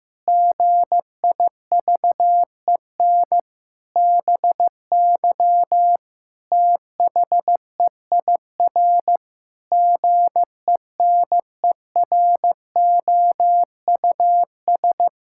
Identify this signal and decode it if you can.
Morse code, heard as a keyed tone, about 15 wpm, 700 Hz. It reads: GIVEN BY THEIR GENEROUS
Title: Pangram Challenge v2 - 15wpm